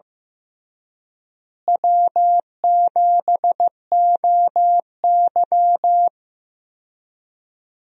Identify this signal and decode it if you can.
Morse code, heard as a keyed tone, about 15 wpm, 700 Hz. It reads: W7OY
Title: USA Call Signs - 4 Characters - Repeated 3x - 15wpm